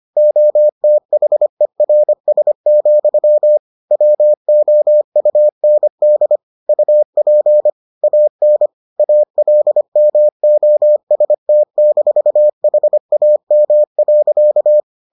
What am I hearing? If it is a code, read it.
Morse code, heard as a keyed tone, about 25 wpm, 600 Hz. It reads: OTHERS, WOUND UP AN ALMOST-HAM.